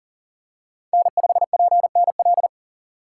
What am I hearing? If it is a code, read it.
Morse code, heard as a keyed tone, about 37 wpm, 700 Hz. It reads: N5PNL
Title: USA Call Signs - 5 Characters - 3x Character Spacing - 40wpm